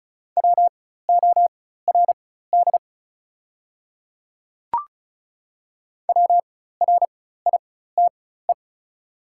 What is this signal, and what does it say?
Morse code, heard as a keyed tone, about 34 wpm, spaced as for 16 wpm, 690 Hz. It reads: WORD WRITE
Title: 4x Character Spacing - Top 100 Words - 35wpm